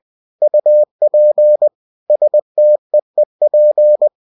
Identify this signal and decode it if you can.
Morse code, heard as a keyed tone, about 20 wpm, 600 Hz. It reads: UP STEEP
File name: cw-books / winnie-the-pooh / winnie-20wpm-page-0028